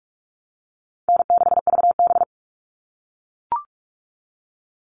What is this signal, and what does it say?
Morse code, heard as a keyed tone, about 44 wpm, 700 Hz. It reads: N6VB
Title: All Cut Numbers - 4-Digit 45wpm